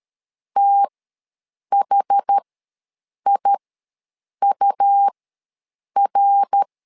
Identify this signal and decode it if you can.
Morse code, heard as a keyed tone, about 13 wpm, spaced as for 7 wpm, 800 Hz. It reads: THIUR
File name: LEZIONI / 3 MWRO / 3 MWR ws 12wpm 1809